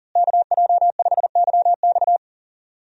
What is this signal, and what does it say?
Morse code, heard as a keyed tone, about 40 wpm, 700 Hz. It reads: KJ5YX
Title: USA Call Signs - 5 Characters - Repeated 2x - 40wpm